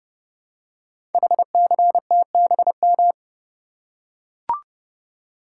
Morse code, heard as a keyed tone, about 30 wpm, 700 Hz. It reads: HCTBM
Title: Code Groups - 5 Letters - 30wpm